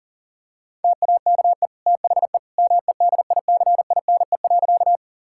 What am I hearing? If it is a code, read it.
Morse code, heard as a keyed tone, about 40 wpm, 700 Hz. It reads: TAKE THE MEDICINE.
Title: Sentences from Top 1000 Words - Review Part 6 - 40wpm